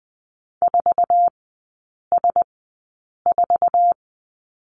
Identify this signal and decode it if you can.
Morse code, heard as a keyed tone, about 20 wpm, spaced as for 8 wpm, 700 Hz. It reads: 4S4
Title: 047 Lesson 9e - Number 4 - Three Characters 20wpm